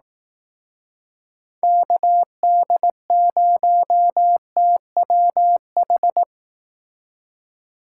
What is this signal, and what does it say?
Morse code, heard as a keyed tone, about 18 wpm, 700 Hz. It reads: KD0TWH
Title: Speed Racing - USA 2x3 Call Signs 15wpm